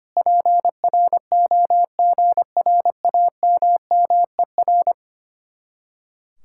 Morse code, heard as a keyed tone, about 33 wpm, 700 Hz. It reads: PROGRAMMER